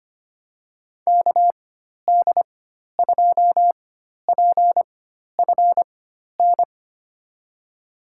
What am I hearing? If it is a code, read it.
Morse code, heard as a keyed tone, about 25 wpm, spaced as for 12 wpm, 700 Hz. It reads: KD2PFN
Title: USA Call Signs - 4x Character Spacing - 25wpm